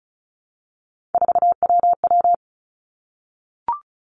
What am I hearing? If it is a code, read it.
Morse code, heard as a keyed tone, about 36 wpm, 700 Hz. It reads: 4WW
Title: Three Challenging Letter-Number Combinations 35wpm